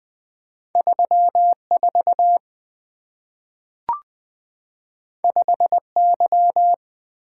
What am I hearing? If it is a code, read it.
Morse code, heard as a keyed tone, about 20 wpm, 700 Hz. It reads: 34 5Y
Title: Two Letter-Number Combinations - 20wpm